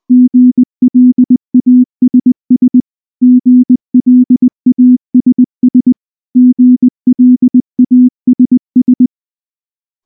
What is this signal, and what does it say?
Morse code, heard as a keyed tone, about 19 wpm, 250 Hz. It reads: GLASS GLASS GLASS
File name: morse / morse-training-20WPM-6